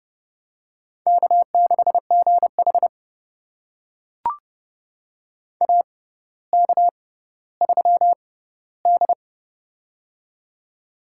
Morse code, heard as a keyed tone, about 29 wpm, spaced as for 15 wpm, 700 Hz. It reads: K6GH AK3D
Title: USA Call Signs - 4 Characters - 6x Character Spacing - 30wpm